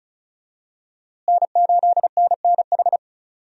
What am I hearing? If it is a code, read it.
Morse code, heard as a keyed tone, about 34 wpm, 700 Hz. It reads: N8NNH
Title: USA Call Signs - 5 Characters - Repeated 2x - 35wpm